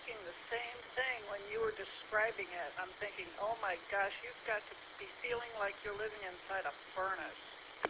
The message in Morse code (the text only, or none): none